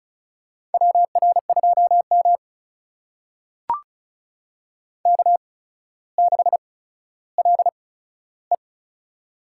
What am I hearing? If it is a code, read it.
Morse code, heard as a keyed tone, about 36 wpm, spaced as for 15 wpm, 700 Hz. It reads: WR2M K6LE
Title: USA Call Signs - 4 Characters - 8x Character Spacing - 35wpm